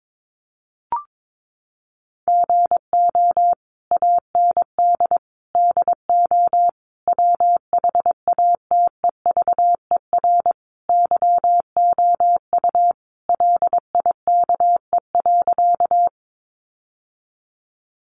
Morse code, heard as a keyed tone, about 22 wpm, 700 Hz. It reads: GO AND DO WHATEVER YOU LIKE.
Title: Sentences from Top 1100 Words 22wpm